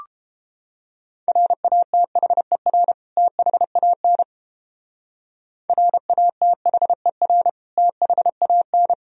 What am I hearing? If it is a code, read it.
Morse code, heard as a keyed tone, about 29 wpm, 700 Hz. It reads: RATHER THAN RATHER THAN